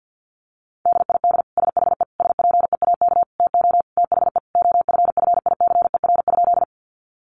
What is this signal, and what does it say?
Morse code, heard as a keyed tone, about 50 wpm, 700 Hz. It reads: DID SHE SPEAK TO THE OFFICER?
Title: Sentences from Top 1200 Words 50wpm